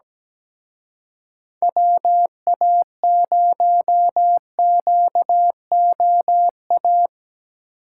Morse code, heard as a keyed tone, about 17 wpm, 700 Hz. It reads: WA0QOA
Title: Speed Racing - USA 2x3 Call Signs 17wpm